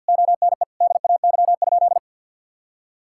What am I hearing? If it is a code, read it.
Morse code, heard as a keyed tone, about 50 wpm, 700 Hz. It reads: ONE DAY?
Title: Sentences from Top 1000 Words - Review Part 6 - 50wpm